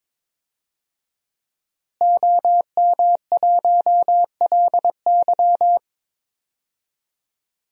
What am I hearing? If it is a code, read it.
Morse code, heard as a keyed tone, about 22 wpm, 700 Hz. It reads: OM1LY
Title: DXCC-100 Call Signs - 5 Characters - Repeated 3x - 22wpm